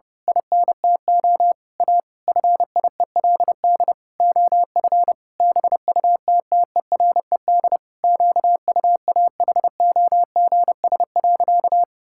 Morse code, heard as a keyed tone, about 30 wpm, 700 Hz. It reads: INTO A FIELD OF BUTTERED QUAHOGS.
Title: Pangram Challenge v2 - 30wpm